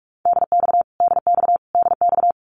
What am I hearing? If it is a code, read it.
Morse code, heard as a keyed tone, about 44 wpm, 700 Hz. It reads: DX DX DX
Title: Abbreviations Repeated 3x - 45wpm